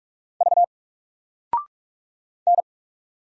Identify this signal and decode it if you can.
Morse code, heard as a keyed tone, about 44 wpm, 700 Hz. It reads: V N
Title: Single Character (Letter:Number:Punctuation:Prosign) Alt 45wpm